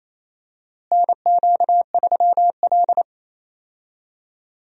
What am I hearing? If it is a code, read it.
Morse code, heard as a keyed tone, about 28 wpm, 700 Hz. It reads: NQ3L
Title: USA Call Signs - 4 Characters - Repeated 2x - 28wpm